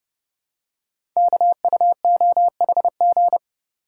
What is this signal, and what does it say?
Morse code, heard as a keyed tone, about 30 wpm, 700 Hz. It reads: KUOHG
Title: Code Groups - 5 Letters - 30wpm